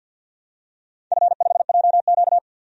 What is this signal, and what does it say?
Morse code, heard as a keyed tone, about 52 wpm, 700 Hz. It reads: F5JQ
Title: DXCC-100 Call Signs - 4 Characters - 50wpm